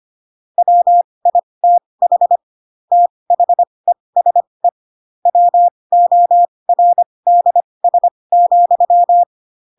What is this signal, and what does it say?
Morse code, heard as a keyed tone, about 25 wpm, spaced as for 20 wpm, 700 Hz. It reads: WITH THESE WORDS,
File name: Pinocchio / Pinocchio ch1 15 25 ik1hge